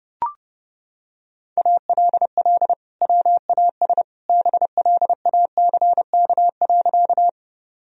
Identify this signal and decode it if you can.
Morse code, heard as a keyed tone, about 32 wpm, 700 Hz. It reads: ALL WAS BLACK.
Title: Sentences from Top 1000 Words - Review Part 2 - 30wpm